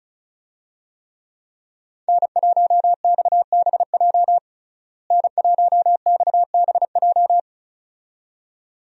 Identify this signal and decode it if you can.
Morse code, heard as a keyed tone, about 36 wpm, 700 Hz. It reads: N1XBJ N1XBJ
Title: USA Call Signs - 5 Characters - Repeated 2x - 35wpm